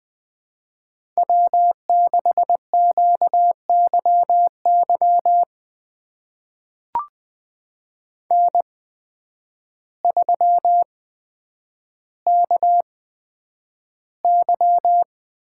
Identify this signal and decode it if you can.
Morse code, heard as a keyed tone, about 20 wpm, spaced as for 9 wpm, 700 Hz. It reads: W6QYY N3KY
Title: USA Call Signs - 5 Characters - 8x Character Spacing - 20wpm